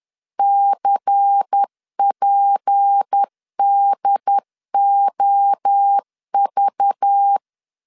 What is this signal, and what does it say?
Morse code, heard as a keyed tone, about 10 wpm, 800 Hz. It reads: CPDOV
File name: LEZIONI / 6 PKX0 / 6 P 10wpm 2432